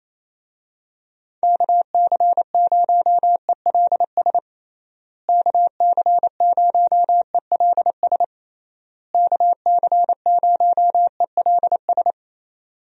Morse code, heard as a keyed tone, about 27 wpm, 700 Hz. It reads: KC0ELS KC0ELS KC0ELS
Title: USA Call Signs - 6 Characters - Repeated 3x - 28wpm